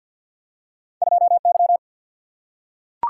Morse code, heard as a keyed tone, about 53 wpm, 700 Hz. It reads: 2Y
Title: Single-Letter Single-Number 50wpm